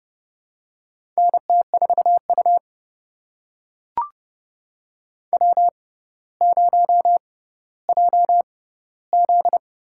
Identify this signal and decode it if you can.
Morse code, heard as a keyed tone, about 30 wpm, spaced as for 15 wpm, 700 Hz. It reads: NT4U W0JZ
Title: USA Call Signs - 6x Character Spacing - 30wpm